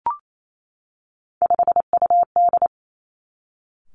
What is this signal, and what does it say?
Morse code, heard as a keyed tone, about 28 wpm, 700 Hz. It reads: 5UD
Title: Three Challenging Letter-Number Combinations 28wpm